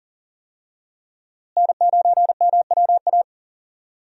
Morse code, heard as a keyed tone, about 40 wpm, 700 Hz. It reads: N9MWA
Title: USA Call Signs - 5 Characters - Repeated 3x - 40wpm